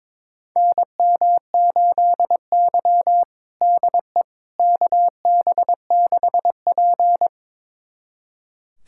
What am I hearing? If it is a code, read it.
Morse code, heard as a keyed tone, about 22 wpm, 700 Hz. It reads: NM8Y DE KB6P
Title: USA Callsign DE Callsign - 4 Characters - 22wpm